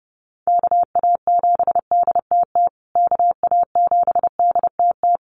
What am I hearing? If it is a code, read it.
Morse code, heard as a keyed tone, about 30 wpm, 700 Hz. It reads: KA7DTT KA7DTT
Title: USA Call Signs Repeated 2x 30wpm